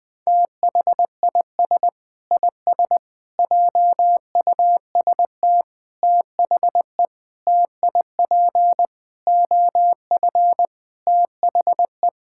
Morse code, heard as a keyed tone, about 20 wpm, 700 Hz. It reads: THIS IS JUST THE TIP OF THE
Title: Sentences with 6-Word Phrases - 20wpm